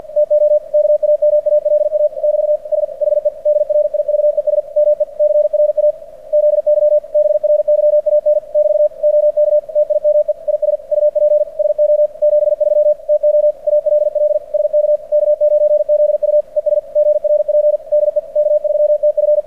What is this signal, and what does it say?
Morse code, heard as a keyed tone, about 46 wpm, 590 Hz. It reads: TO KAMALA <BT> SHE DRIVES ME CRAZY BRYAN <BT> OK TIME IS UP SO 73 TO ALL SK K9ZU ES GRP DE K3TF
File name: archived / CRSnet